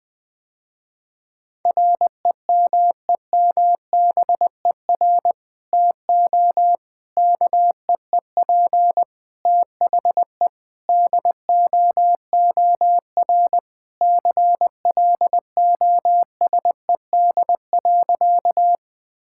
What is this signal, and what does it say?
Morse code, heard as a keyed tone, about 21 wpm, 700 Hz. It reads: REMEMBER TO KEEP THE DOOR CLOSED.
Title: Sentences with 3-Word Phrases - Spoken Beginning - 20wpm